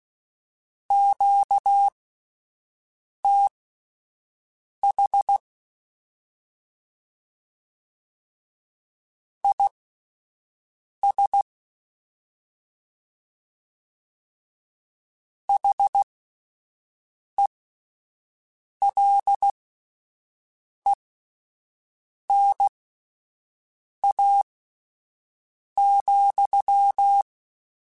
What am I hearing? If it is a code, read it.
Morse code, heard as a keyed tone, about 16 wpm, spaced as for 5 wpm, 800 Hz. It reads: QTH IS HELENA,
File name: (spaced) / QSOs Spaced-mp3s / WD0BGP 1 @10 sp